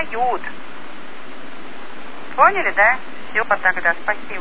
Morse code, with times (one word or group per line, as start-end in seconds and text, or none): none